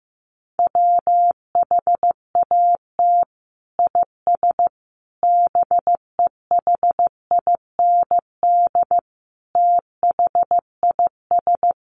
Morse code, heard as a keyed tone, about 15 wpm, 700 Hz. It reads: WHAT IS BEHIND THIS
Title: Sentences from Top 1100 Words 15wpm